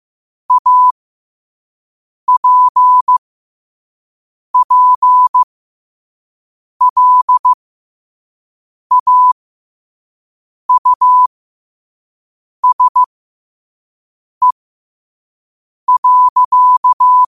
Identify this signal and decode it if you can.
Morse code, heard as a keyed tone, about 15 wpm, spaced as for 5 wpm, 1000 Hz. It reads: APPLAUSE.
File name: qotdfiles / 5wpm